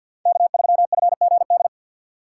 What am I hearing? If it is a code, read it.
Morse code, heard as a keyed tone, about 54 wpm, 700 Hz. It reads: K3FGD
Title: USA Call Signs - 5 Characters - 8x Character Spacing - 50wpm